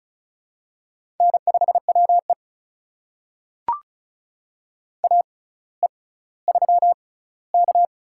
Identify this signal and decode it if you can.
Morse code, heard as a keyed tone, about 34 wpm, spaced as for 18 wpm, 700 Hz. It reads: N5WE AE3K
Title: USA Call Signs - 4 Characters - 6x Character Spacing - 35wpm